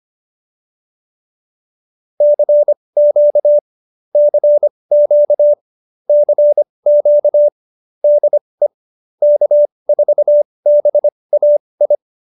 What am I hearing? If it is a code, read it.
Morse code, heard as a keyed tone, about 25 wpm, spaced as for 20 wpm, 590 Hz. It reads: CQ CQ CQ DE K4BAI
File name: QSO301 20